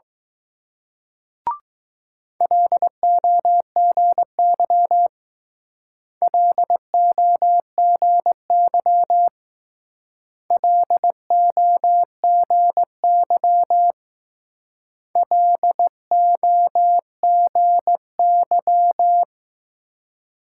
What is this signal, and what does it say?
Morse code, heard as a keyed tone, about 19 wpm, 700 Hz. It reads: LOGY LOGY LOGY LOGY